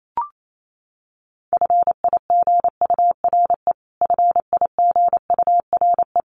Morse code, heard as a keyed tone, about 28 wpm, 700 Hz. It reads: FIGURE FIGURE
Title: Top 500 Words Repeated 28wpm